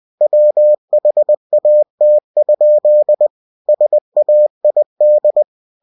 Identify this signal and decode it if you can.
Morse code, heard as a keyed tone, about 20 wpm, 600 Hz. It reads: WHAT? SAID
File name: cw-books / winnie-the-pooh / winnie-20wpm-page-0083